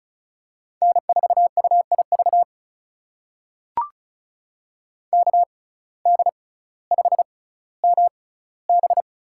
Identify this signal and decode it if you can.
Morse code, heard as a keyed tone, about 36 wpm, spaced as for 18 wpm, 700 Hz. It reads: N4UIV KD5MB
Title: USA Call Signs - 5 Characters - 6x Character Spacing - 35wpm